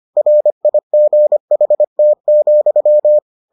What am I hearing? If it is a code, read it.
Morse code, heard as a keyed tone, about 25 wpm, 600 Hz. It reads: RIGHT,